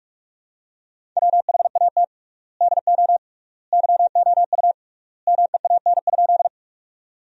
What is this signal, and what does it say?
Morse code, heard as a keyed tone, about 43 wpm, 700 Hz. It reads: WHAT DO YOU MEAN?
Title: 3x Word-Spacing - Sentences from Top 200 Words - 45wpm